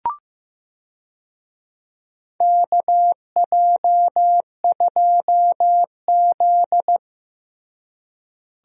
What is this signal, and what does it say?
Morse code, heard as a keyed tone, about 15 wpm, 700 Hz. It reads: KJ2Z
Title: USA Call Signs - 4 Characters - Repeated 2x - 15wpm